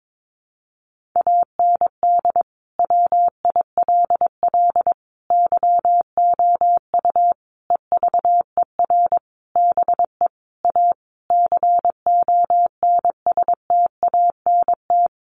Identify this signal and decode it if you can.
Morse code, heard as a keyed tone, about 22 wpm, 700 Hz. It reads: AND WILL YOU EVER BE A CONSTANT
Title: Sentences from Top 1500 Words 22wpm